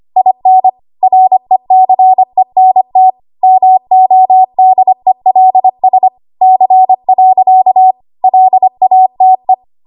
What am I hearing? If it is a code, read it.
Morse code, heard as a keyed tone, about 25 wpm, 750 Hz. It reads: IN RECENT MODELS C. LATE
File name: Archive / 25 WPM / 25WPM